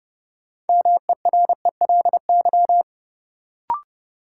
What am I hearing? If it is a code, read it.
Morse code, heard as a keyed tone, about 30 wpm, 700 Hz. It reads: MERELY
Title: Sentences from Top 1400 Words - New Word Drill Alt - 30wpm